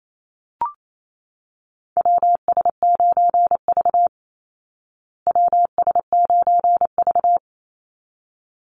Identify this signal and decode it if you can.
Morse code, heard as a keyed tone, about 28 wpm, 700 Hz. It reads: WS9V WS9V
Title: Call Signs Repeated 3x - 28wpm